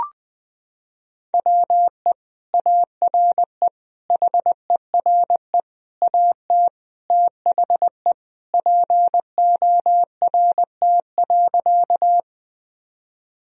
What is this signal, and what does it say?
Morse code, heard as a keyed tone, about 21 wpm, 700 Hz. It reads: WE ARE HERE AT THE PORT.
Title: Sentences from Top 1000 Words - Review Part 1 - 20wpm